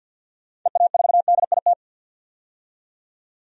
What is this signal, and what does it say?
Morse code, heard as a keyed tone, about 51 wpm, 700 Hz. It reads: EA4DIT
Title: DXCC-100 Call Signs - 6 Characters - 50wpm